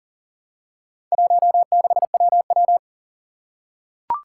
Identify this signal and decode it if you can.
Morse code, heard as a keyed tone, about 40 wpm, 700 Hz. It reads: 16WW
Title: Four Challenging Letter-Number Combinations 40wpm